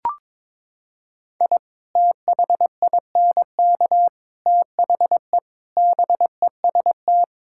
Morse code, heard as a keyed tone, about 22 wpm, 700 Hz. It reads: I THINK THE BEST
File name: practice / 22wpm / Sentences with 8-Word Phrases - 22wpm